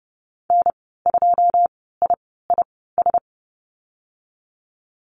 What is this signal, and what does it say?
Morse code, heard as a keyed tone, about 30 wpm, spaced as for 17 wpm, 700 Hz. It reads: N2IIS
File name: Lesson 15g - Number 5 - Call Signs 30wpm